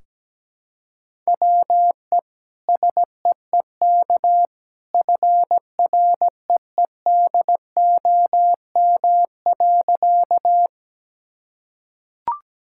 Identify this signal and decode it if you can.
Morse code, heard as a keyed tone, about 17 wpm, 700 Hz. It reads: WE SEEK FREEDOM.
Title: Sentences from Top 1600 Words 17wpm